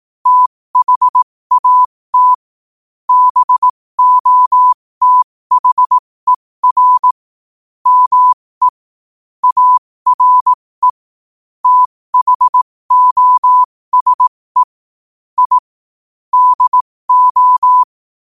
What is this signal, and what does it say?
Morse code, heard as a keyed tone, about 18 wpm, spaced as for 15 wpm, 1000 Hz. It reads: THAT BOTHER ME ARE THOSE I DO
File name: qotdfiles / 15wpm